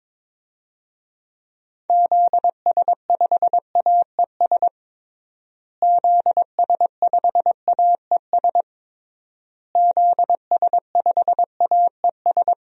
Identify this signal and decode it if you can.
Morse code, heard as a keyed tone, about 22 wpm, 700 Hz. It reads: ZS5AES ZS5AES ZS5AES
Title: DXCC-100 Call Signs - 6 Characters - Repeated 3x - 22wpm